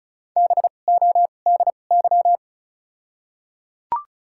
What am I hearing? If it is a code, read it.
Morse code, heard as a keyed tone, about 36 wpm, spaced as for 25 wpm, 700 Hz. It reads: BODY
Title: 2x Character Spacing - Top 500 Words - 35wpm